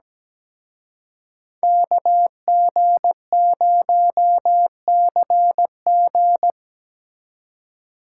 Morse code, heard as a keyed tone, about 20 wpm, 700 Hz. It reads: KG0CG
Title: Speed Racing - USA Call Signs 17wpm